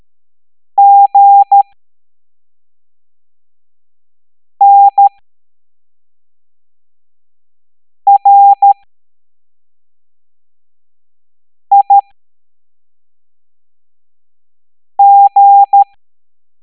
Morse code, heard as a keyed tone, about 13 wpm, spaced as for 3 wpm, 800 Hz. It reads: GNRIG
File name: hamdata / L03-EX01-IGC